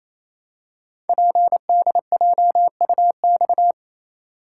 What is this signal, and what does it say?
Morse code, heard as a keyed tone, about 27 wpm, 700 Hz. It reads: PDJUX
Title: Code Groups - 5 Letters - 28wpm